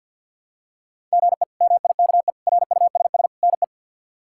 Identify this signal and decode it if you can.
Morse code, heard as a keyed tone, about 52 wpm, 700 Hz. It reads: GE MIKE RUSS NE
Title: K1USN SST QSO Practice - 50wpm